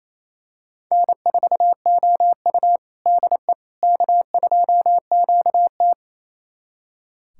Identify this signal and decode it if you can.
Morse code, heard as a keyed tone, about 28 wpm, 700 Hz. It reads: N4OU DE K2QT
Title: USA Callsign DE Callsign - 4 Characters - 28wpm